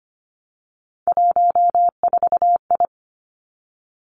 Morse code, heard as a keyed tone, about 25 wpm, 700 Hz. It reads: 14I
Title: Three Challenging Letter-Number Combinations 25wpm